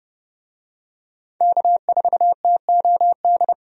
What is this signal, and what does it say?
Morse code, heard as a keyed tone, about 30 wpm, 700 Hz. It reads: K4TOD
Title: USA Call Signs - 5 Characters - 30wpm